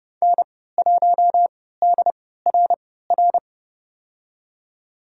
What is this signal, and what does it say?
Morse code, heard as a keyed tone, about 30 wpm, spaced as for 17 wpm, 700 Hz. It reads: N1DRR
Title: USA Call Signs - 5 Characters - 3x Character Spacing - 30wpm